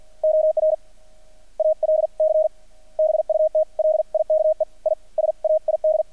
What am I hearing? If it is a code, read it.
Morse code, heard as a keyed tone, about 50 wpm, spaced as for 44 wpm, 620 Hz. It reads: OR ALC BUT LIKE I SAID
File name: archived / CRSnet